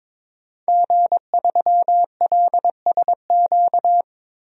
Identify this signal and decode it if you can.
Morse code, heard as a keyed tone, about 21 wpm, 700 Hz. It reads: G3LSQ
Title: DXCC-100 Call Signs - 5 Characters - 22wpm